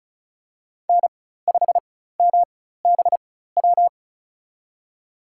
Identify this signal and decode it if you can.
Morse code, heard as a keyed tone, about 34 wpm, spaced as for 16 wpm, 700 Hz. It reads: N5MBW
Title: USA Call Signs - 5 Characters - 4x Character Spacing - 35wpm